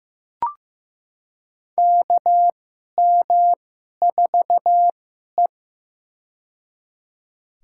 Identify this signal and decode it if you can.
Morse code, heard as a keyed tone, about 15 wpm, spaced as for 11 wpm, 700 Hz. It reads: KM4E
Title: USA Call Signs - 4 Characters - 2x Character Spacing - 15wpm